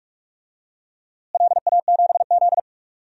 Morse code, heard as a keyed tone, about 46 wpm, 690 Hz. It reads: LA7Z
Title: DXCC-100 Call Signs - 4 Characters - Repeated 3x - 45wpm